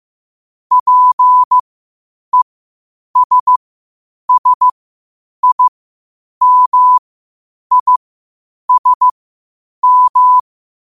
Morse code, heard as a keyed tone, about 15 wpm, spaced as for 8 wpm, 1000 Hz. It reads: PESSIMISM
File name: qotdfiles / 8wpm